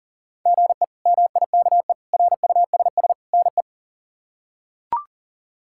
Morse code, heard as a keyed tone, about 40 wpm, 690 Hz. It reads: GE MIKE RUSS NE
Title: K1USN SST QSO Practice - 40wpm